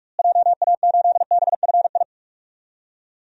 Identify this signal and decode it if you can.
Morse code, heard as a keyed tone, about 46 wpm, 700 Hz. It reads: JA8BFI
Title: DXCC-100 Call Signs - 6 Characters - Repeated 3x - 45wpm